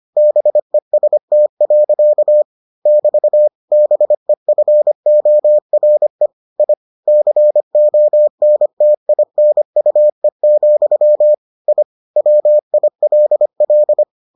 Morse code, heard as a keyed tone, about 25 wpm, 600 Hz. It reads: BEST. <BT> BEFORE I CONTINUE, I WILL